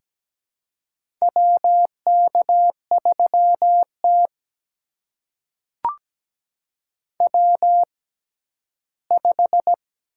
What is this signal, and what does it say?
Morse code, heard as a keyed tone, about 17 wpm, spaced as for 12 wpm, 700 Hz. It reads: WK3T W5
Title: USA Call Signs - 4 Characters - 6x Character Spacing - 17wpm